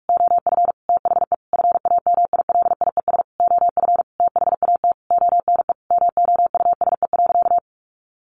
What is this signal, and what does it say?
Morse code, heard as a keyed tone, about 44 wpm, 700 Hz. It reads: OF THE FAMILIES OF THAT ONE MOUSE.
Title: A Sound Of Thunder 45wpm